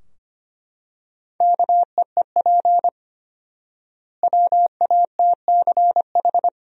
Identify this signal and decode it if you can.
Morse code, heard as a keyed tone, about 25 wpm, 700 Hz. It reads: KEEP WATCH
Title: 4x Word-Spacing - Sentences from Top 300 Words - 25wpm